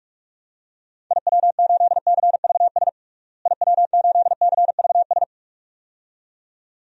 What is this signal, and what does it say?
Morse code, heard as a keyed tone, about 46 wpm, 700 Hz. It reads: IW8CVS IW8CVS
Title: DXCC-100 Call Signs - 6 Characters - Repeated 2x - 45wpm